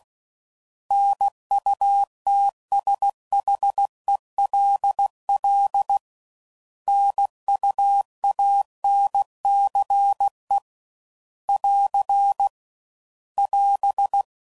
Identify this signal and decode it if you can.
Morse code, heard as a keyed tone, about 16 wpm, 800 Hz. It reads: NUTSHELL NUANCE <AR> <AS>
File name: Prefixes / By First Letter / Ns